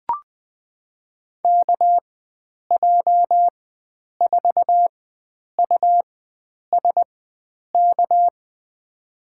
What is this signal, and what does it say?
Morse code, heard as a keyed tone, about 21 wpm, spaced as for 9 wpm, 700 Hz. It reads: KJ4USK
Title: USA Call Signs - 6 Characters - 4x Character Spacing - 20wpm